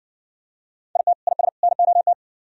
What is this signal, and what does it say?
Morse code, heard as a keyed tone, about 59 wpm, 700 Hz. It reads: IT IS NOT